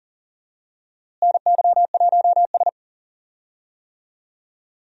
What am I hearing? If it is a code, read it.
Morse code, heard as a keyed tone, about 40 wpm, 700 Hz. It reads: NY1S
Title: USA Call Signs - 4 Characters - Repeated 3x - 40wpm